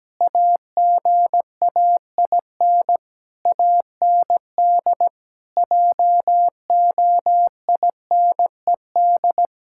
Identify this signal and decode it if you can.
Morse code, heard as a keyed tone, about 17 wpm, 700 Hz. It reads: AGAIN AND JOINED
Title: Pangram Challenge v2 - 17wpm